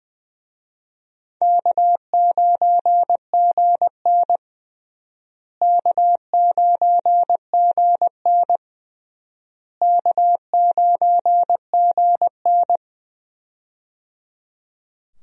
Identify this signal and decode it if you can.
Morse code, heard as a keyed tone, about 20 wpm, 700 Hz. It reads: K9GN K9GN K9GN
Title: USA Call Signs - 4 Characters - Repeated 3x - 20wpm